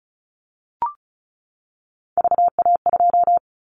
Morse code, heard as a keyed tone, about 34 wpm, 700 Hz. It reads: VA2
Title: Callsign Prefixes for Top 100 DXCC - 35wpm